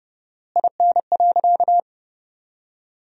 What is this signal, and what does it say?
Morse code, heard as a keyed tone, about 28 wpm, 700 Hz. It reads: IN.
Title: 4x Word-Spacing - Sentences from Top 400 Words - 30wpm